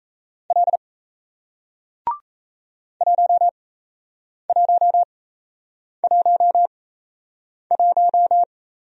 Morse code, heard as a keyed tone, about 38 wpm, spaced as for 8 wpm, 700 Hz. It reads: L 1111